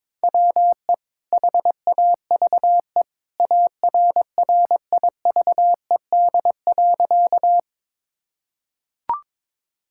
Morse code, heard as a keyed tone, about 22 wpm, 700 Hz. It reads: WE HAVE ARRIVED.
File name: practice / 22wpm / Sentences with 2-Word Phrases - 22wpm